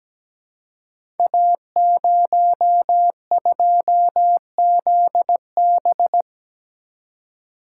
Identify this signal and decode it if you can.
Morse code, heard as a keyed tone, about 17 wpm, 700 Hz. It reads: A02ZB
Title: Code Groups - 5 Letter:Numbers - 17wpm